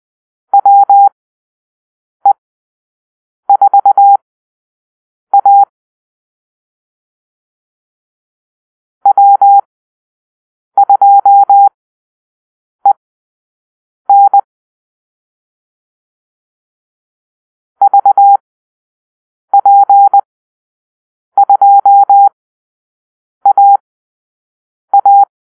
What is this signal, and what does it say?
Morse code, heard as a keyed tone, about 20 wpm, spaced as for 6 wpm, 800 Hz. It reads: WE4A W2EN VP2AA
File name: arrl / Call-Sign Drill-6